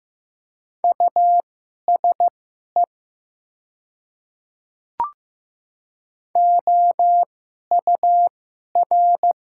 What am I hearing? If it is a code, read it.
Morse code, heard as a keyed tone, about 15 wpm, spaced as for 11 wpm, 700 Hz. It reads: USE OUR
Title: 2x Character Spacing - Top 300 Words - 15wpm